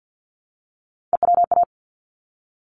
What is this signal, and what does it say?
Morse code, heard as a keyed tone, about 50 wpm, 700 Hz. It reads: EWA
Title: ARRL Sections 50wpm